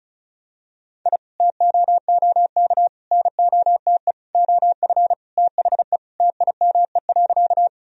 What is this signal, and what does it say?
Morse code, heard as a keyed tone, about 36 wpm, 700 Hz. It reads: I TOOK NOTE OF THE TIME.
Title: Sentences from Top 1000 Words - Review Part 2 - 35wpm